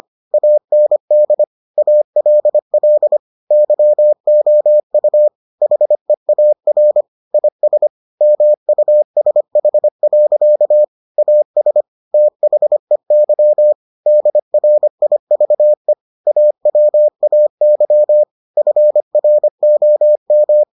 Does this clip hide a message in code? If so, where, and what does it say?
Yes: Morse code from 0.3 to 20.6 s, heard as a keyed tone, about 25 wpm, 600 Hz. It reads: AND ALL YOU HEAR IS MUSH. AS THEY DRIVE AWAY FROM